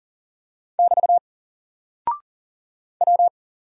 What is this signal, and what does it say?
Morse code, heard as a keyed tone, about 40 wpm, 700 Hz. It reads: <BT> W
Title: Single Character (Letter:Number:Punctuation:Prosign) Alt 40wpm